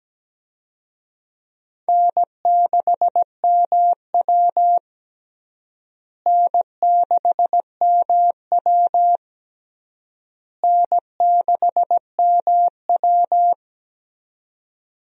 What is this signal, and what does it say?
Morse code, heard as a keyed tone, about 17 wpm, 700 Hz. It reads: N6MW N6MW N6MW
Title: USA Call Signs - 4 Characters - Repeated 3x - 17wpm